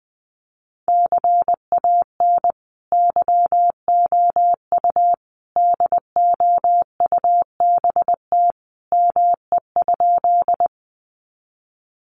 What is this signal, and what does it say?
Morse code, heard as a keyed tone, about 20 wpm, 700 Hz. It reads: CAN YOU DOUBT ME?
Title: Sentences from Top 1500 Words 20wpm